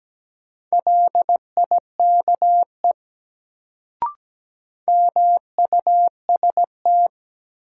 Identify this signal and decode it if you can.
Morse code, heard as a keyed tone, about 17 wpm, 700 Hz. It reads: LIKE MUST
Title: Sets of 2 Words - New Word Drill - 17wpm